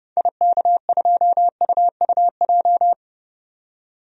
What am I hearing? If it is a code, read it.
Morse code, heard as a keyed tone, about 30 wpm, 700 Hz. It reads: IK2UUJ
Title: DXCC-100 Call Signs - 6 Characters - 30wpm